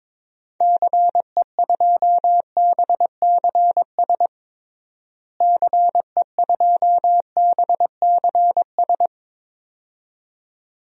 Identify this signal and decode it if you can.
Morse code, heard as a keyed tone, about 21 wpm, 700 Hz. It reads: CE2BCS CE2BCS
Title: DXCC-100 Call Signs - 6 Characters - Repeated 2x - 22wpm